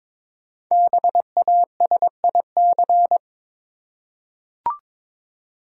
Morse code, heard as a keyed tone, about 22 wpm, 700 Hz. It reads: BASIC
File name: 5 Letter Words - First 3 Letters Given - 22wpm